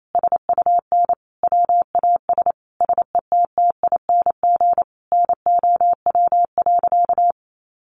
Morse code, heard as a keyed tone, about 28 wpm, 700 Hz. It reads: SUN WAS SETTING NOW.
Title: Sentences from Top 1300 Words 28wpm